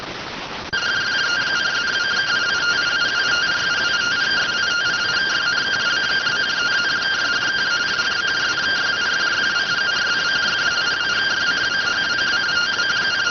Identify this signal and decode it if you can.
Two-tone teleprinter signal, about 170 Hz shift.